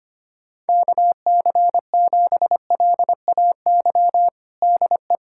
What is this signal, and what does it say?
Morse code, heard as a keyed tone, about 25 wpm, 700 Hz. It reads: KC7LAY DE
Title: USA Callsign DE Callsign - 6 Characters - 25wpm